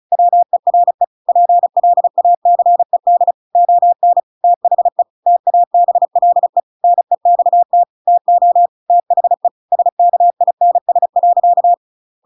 Morse code, heard as a keyed tone, about 36 wpm, 700 Hz. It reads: WERE PLACED ON THE TABLE NEXT TO THE SKINS.
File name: Pinocchio / Pinocchio ch7 35 35 ik1hge